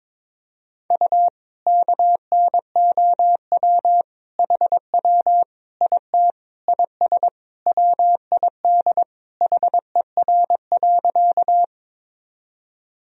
Morse code, heard as a keyed tone, about 22 wpm, 700 Hz. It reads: U KNOW HW IT IS WID HER.
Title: Sentences from Top 100 Words and Abbreviations 22wpm